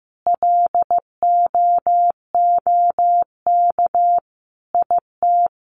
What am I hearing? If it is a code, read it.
Morse code, heard as a keyed tone, about 15 wpm, 700 Hz. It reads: LOOK IT
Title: Sentences from Top 1200 Words 15wpm